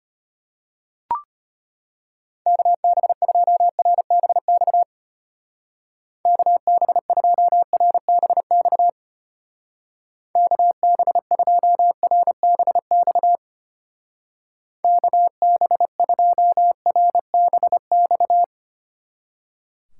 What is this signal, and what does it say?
Morse code, heard as a keyed tone, about 32 wpm, 700 Hz. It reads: KB2RBX KB2RBX KB2RBX KB2RBX